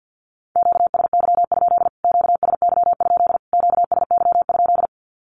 Morse code, heard as a keyed tone, about 49 wpm, 700 Hz. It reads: QSY? QSY? QSY?
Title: Q-Codes Repeated 3x 50wpm